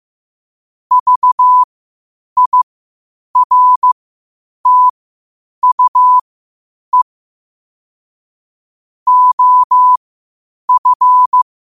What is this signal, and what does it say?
Morse code, heard as a keyed tone, about 15 wpm, spaced as for 8 wpm, 1000 Hz. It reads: VIRTUE OF
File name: qotdfiles / 8wpm